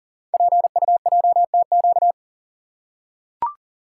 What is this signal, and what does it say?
Morse code, heard as a keyed tone, about 39 wpm, 700 Hz. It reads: PUJTQ
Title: Code Groups - 5 Letter:Numbers - 40wpm